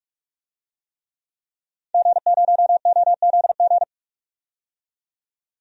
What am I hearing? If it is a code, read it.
Morse code, heard as a keyed tone, about 41 wpm, 700 Hz. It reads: G0OZG
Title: DXCC-100 Call Signs - 5 Characters - 45wpm